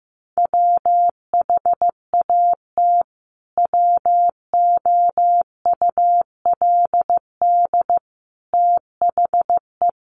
Morse code, heard as a keyed tone, about 15 wpm, 700 Hz. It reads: WHAT WOULD THE
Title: Sentences from Top 1300 Words 15wpm